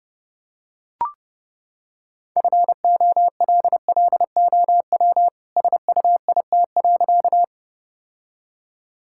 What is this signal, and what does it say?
Morse code, heard as a keyed tone, about 30 wpm, 700 Hz. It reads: FOLLOW SUIT.
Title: Sentences from Top 1600 Words 30wpm